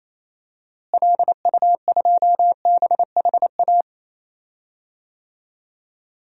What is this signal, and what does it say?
Morse code, heard as a keyed tone, about 27 wpm, 700 Hz. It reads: LU2BHA
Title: DXCC-100 Call Signs - 6 Characters - Repeated 2x - 28wpm